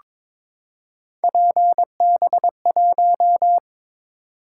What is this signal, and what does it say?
Morse code, heard as a keyed tone, about 22 wpm, 700 Hz. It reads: PB1
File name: practice / 22wpm / Three Letter-Number Combinations - 22wpm